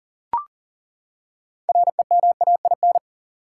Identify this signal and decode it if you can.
Morse code, heard as a keyed tone, about 39 wpm, 700 Hz. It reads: REMAIN
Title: Six Letter Words - 40wpm